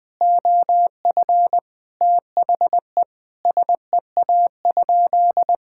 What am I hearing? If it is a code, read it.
Morse code, heard as a keyed tone, about 21 wpm, 700 Hz. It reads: OF THE SEA?
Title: Sentences from Top 1000 Words - Review Part 6 - 20wpm